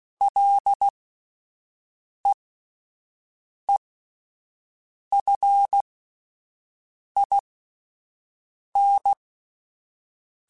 Morse code, heard as a keyed tone, about 16 wpm, spaced as for 5 wpm, 800 Hz. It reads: LEEFIN
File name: Ls / Single L m3s / Single L Prefixe 3x2 Twice @5